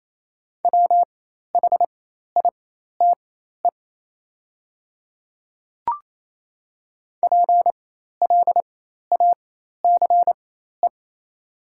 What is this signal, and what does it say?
Morse code, heard as a keyed tone, about 27 wpm, spaced as for 13 wpm, 700 Hz. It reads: WHITE PLACE
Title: 4x Character Spacing - Top 300 Words - 28wpm